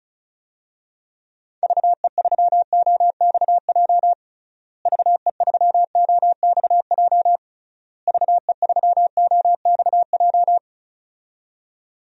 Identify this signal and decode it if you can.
Morse code, heard as a keyed tone, about 36 wpm, 700 Hz. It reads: VE3OXJ VE3OXJ VE3OXJ
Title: DXCC-100 Call Signs - 6 Characters - Repeated 3x - 35wpm